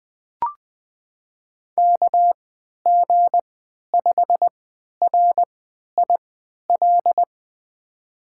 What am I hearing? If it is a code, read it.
Morse code, heard as a keyed tone, about 20 wpm, spaced as for 11 wpm, 700 Hz. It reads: KG5RIL
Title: USA Call Signs - 6 Characters - 3x Character Spacing - 20wpm